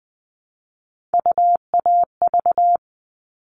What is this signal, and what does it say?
Morse code, heard as a keyed tone, about 20 wpm, 700 Hz. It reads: UAV
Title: All Cut Numbers - 3-Digit 20wpm